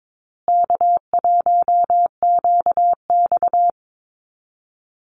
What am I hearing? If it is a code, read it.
Morse code, heard as a keyed tone, about 22 wpm, 700 Hz. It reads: K1QX
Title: Easy Call Signs Repeated 3x - 22wpm